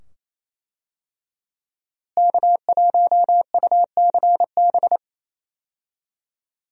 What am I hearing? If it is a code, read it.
Morse code, heard as a keyed tone, about 27 wpm, 700 Hz. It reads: K1UCB
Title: USA Call Signs - 5 Characters - Repeated 2x - 28wpm